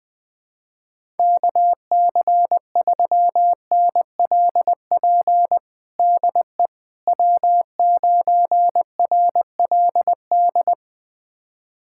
Morse code, heard as a keyed tone, about 20 wpm, 700 Hz. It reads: KC3NLP DE W9RLD
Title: USA Callsign DE Callsign 20wpm